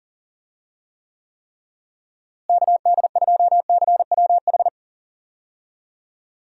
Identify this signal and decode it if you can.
Morse code, heard as a keyed tone, about 40 wpm, 700 Hz. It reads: KD2CWH
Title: USA Call Signs - 6 Characters - Repeated 3x - 40wpm